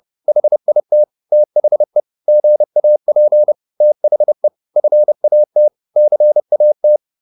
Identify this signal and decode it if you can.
Morse code, heard as a keyed tone, about 30 wpm, 600 Hz. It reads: HIT THE GAP THE FAT CAT
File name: PR204 30